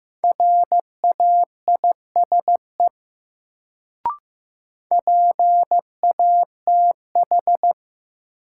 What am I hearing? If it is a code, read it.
Morse code, heard as a keyed tone, about 15 wpm, 700 Hz. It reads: RAISE PATH
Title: Sentences from Top 1400 Words - New Word Drill Alt - 15wpm